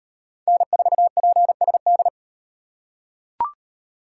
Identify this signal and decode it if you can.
Morse code, heard as a keyed tone, about 38 wpm, 700 Hz. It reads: N4PSD